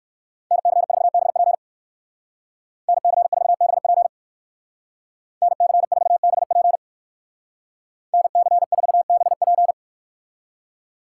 Morse code, heard as a keyed tone, about 59 wpm, 700 Hz. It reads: NC4BP NC4BP NC4BP NC4BP